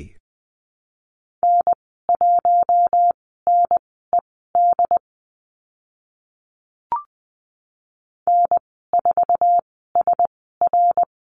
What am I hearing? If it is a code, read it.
Morse code, heard as a keyed tone, about 20 wpm, spaced as for 15 wpm, 700 Hz. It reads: N1NED N4SR
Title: 069 Lesson 12g - Letter d - Call Signs 20wpm